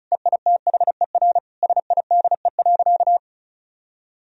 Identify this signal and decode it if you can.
Morse code, heard as a keyed tone, about 36 wpm, 700 Hz. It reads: EITHER SIDE.
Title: Sentences from Top 1000 Words - Review Part 4 - 35wpm